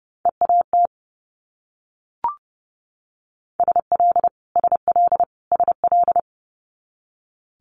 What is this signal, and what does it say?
Morse code, heard as a keyed tone, about 32 wpm, 700 Hz. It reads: EAT SL SL SL
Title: Multi-letter Phonemes Repeated 3x 30wpm